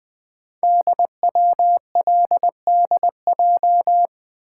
Morse code, heard as a keyed tone, about 20 wpm, 700 Hz. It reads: DWLDJ
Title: Code Groups - 5 Letters - 20wpm